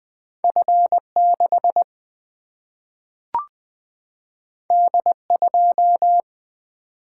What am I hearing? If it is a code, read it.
Morse code, heard as a keyed tone, about 20 wpm, 700 Hz. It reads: F 6 D 2
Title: Single-Letter Single-Number 20wpm